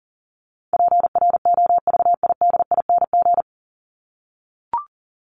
Morse code, heard as a keyed tone, about 40 wpm, 700 Hz. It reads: PROVIDING
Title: ING Suffix - Top 100 Words with Base Words 40wpm